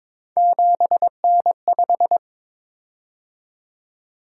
Morse code, heard as a keyed tone, about 22 wpm, 700 Hz. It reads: 7N5
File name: practice / 22wpm / Three Letter-Number Combinations - 22wpm